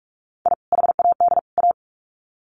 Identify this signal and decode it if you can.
Morse code, heard as a keyed tone, about 44 wpm, 700 Hz. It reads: I HAD A
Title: Sets of 3 Words 45wpm